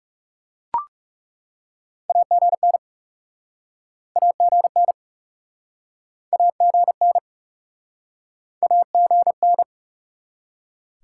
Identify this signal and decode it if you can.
Morse code, heard as a keyed tone, about 41 wpm, 700 Hz. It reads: AGN AGN AGN AGN